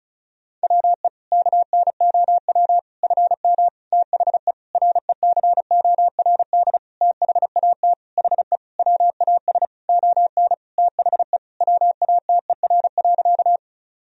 Morse code, heard as a keyed tone, about 36 wpm, 700 Hz. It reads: WE KNOW FM THE RECORD THAT HE WAS ON THE WATER.
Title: Sentences from Top 500 Words and Abbreviations 35wpm